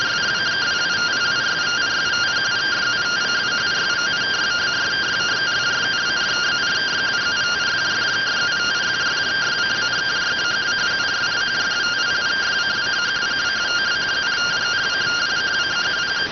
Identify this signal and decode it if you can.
Two-tone teleprinter signal, about 170 Hz shift.